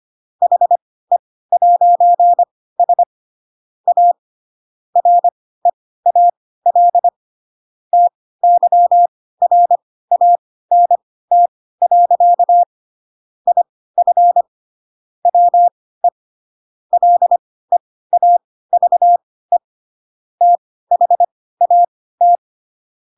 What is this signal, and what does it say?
Morse code, heard as a keyed tone, about 25 wpm, spaced as for 16 wpm, 700 Hz. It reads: HE'S A REAL TYRANT. IF WE LEAVE THAT
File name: Pinocchio / Pinocchio ch3 10 25 ik1hge